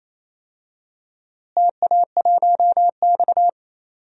Morse code, heard as a keyed tone, about 28 wpm, 700 Hz. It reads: TA1X
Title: DXCC-100 Call Signs - 4 Characters - Repeated 3x - 28wpm